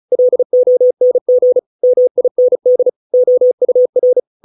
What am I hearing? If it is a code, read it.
Morse code, heard as a keyed tone, about 34 wpm, 500 Hz. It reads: LONG MIND OUR